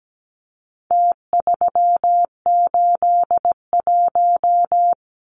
Morse code, heard as a keyed tone, about 17 wpm, 700 Hz. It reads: T381
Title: Numbers - Contest Style - 4-Digit 17wpm